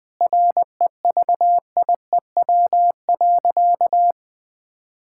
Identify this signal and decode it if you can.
Morse code, heard as a keyed tone, about 20 wpm, 700 Hz. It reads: REVIEW.
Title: Sentences from Top 1000 Words - Review Part 6 - 20wpm